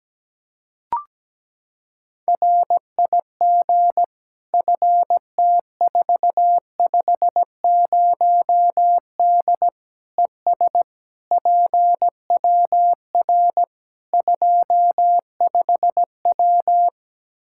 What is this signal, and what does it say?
Morse code, heard as a keyed tone, about 17 wpm, 700 Hz. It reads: RIG FT450D ES PWR 25W
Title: K1USN SST QSO Practice 17wpm